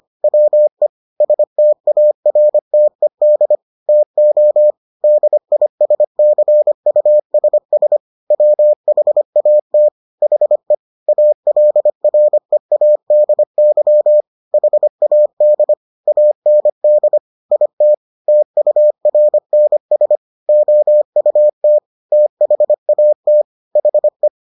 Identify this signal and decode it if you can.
Morse code, heard as a keyed tone, about 25 wpm, 600 Hz. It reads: WE STARTED TO DISCUSS WHAT HE ALREADY HAD AND IT TURNS OUT THAT HE